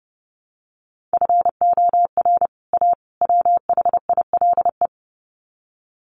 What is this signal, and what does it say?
Morse code, heard as a keyed tone, about 30 wpm, 700 Hz. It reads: FOR A WHILE
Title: Sets of 3 Words 30wpm